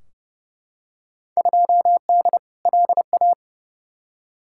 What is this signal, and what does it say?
Morse code, heard as a keyed tone, about 30 wpm, 700 Hz. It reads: 2D LA
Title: Field Day Exchanges - 30wpm